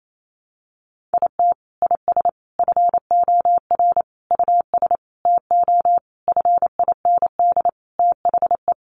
Morse code, heard as a keyed tone, about 28 wpm, 700 Hz. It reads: IT IS FOR US TO FIND THE
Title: Sentences from Top 1300 Words 28wpm